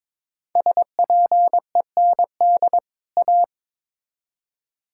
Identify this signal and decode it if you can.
Morse code, heard as a keyed tone, about 22 wpm, 700 Hz. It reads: SPEND A
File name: practice / 22wpm / Build A Phrase - 5-Word Phrases - 22wpm